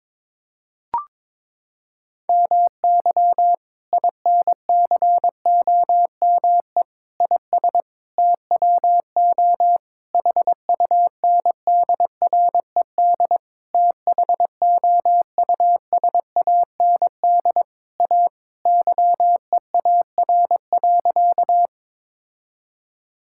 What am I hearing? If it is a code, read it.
Morse code, heard as a keyed tone, about 22 wpm, 700 Hz. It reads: MY INCOME IS TWO HUNDRED THOUSAND A YEAR.
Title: Sentences from Top 1000 Words - Review Part 5 - 22wpm